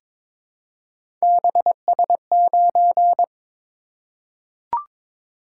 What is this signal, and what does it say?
Morse code, heard as a keyed tone, about 22 wpm, 700 Hz. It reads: BS9
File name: practice / 22wpm / Three Letter-Number Combinations - 22wpm